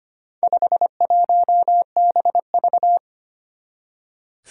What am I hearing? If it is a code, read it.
Morse code, heard as a keyed tone, about 25 wpm, 700 Hz. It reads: 51BV
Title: Four Challenging Letter-Number Combinations 25wpm